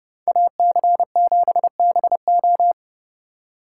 Morse code, heard as a keyed tone, about 29 wpm, 700 Hz. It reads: AC7BO